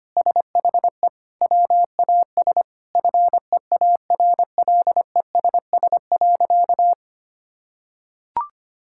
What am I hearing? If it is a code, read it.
Morse code, heard as a keyed tone, about 25 wpm, 700 Hz. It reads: SHE WAS FEARLESS.
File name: practice / 25wpm / Sentences with 4-Word Phrases - 25wpm